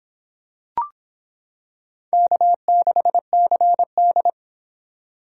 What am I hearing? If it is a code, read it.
Morse code, heard as a keyed tone, about 25 wpm, 700 Hz. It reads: K6CD